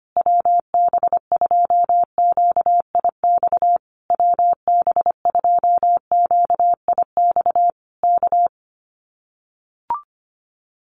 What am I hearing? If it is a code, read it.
Morse code, heard as a keyed tone, about 25 wpm, 700 Hz. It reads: WB2QIX WB2QIX K
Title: Common QSO Phrases Encore 25wpm